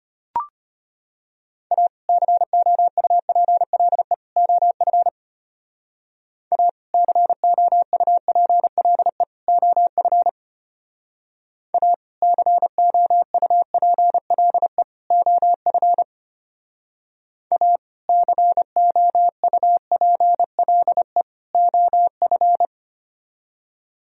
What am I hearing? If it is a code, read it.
Morse code, heard as a keyed tone, about 33 wpm, 700 Hz. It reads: A COUPLE OF A COUPLE OF A COUPLE OF A COUPLE OF